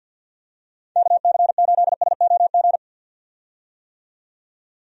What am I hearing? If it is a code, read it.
Morse code, heard as a keyed tone, about 46 wpm, 700 Hz. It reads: KC8SOG
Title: USA Call Signs - 6 Characters - 50wpm